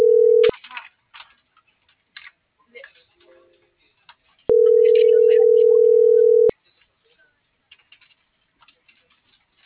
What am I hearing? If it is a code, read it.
Ringback tone (North America), 440 + 480 Hz.